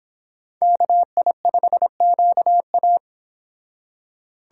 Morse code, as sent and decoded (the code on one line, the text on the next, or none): -.- .. ..... --.- .-
KI5QA